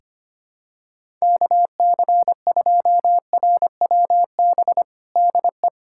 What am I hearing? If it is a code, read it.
Morse code, heard as a keyed tone, about 25 wpm, 700 Hz. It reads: KC2RWB DE
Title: USA Callsign DE Callsign 25wpm